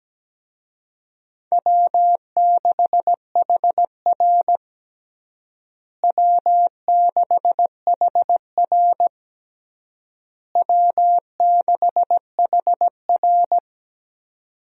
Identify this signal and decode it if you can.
Morse code, heard as a keyed tone, about 17 wpm, 700 Hz. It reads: W6HR W6HR W6HR
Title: USA Call Signs - 4 Characters - Repeated 3x - 17wpm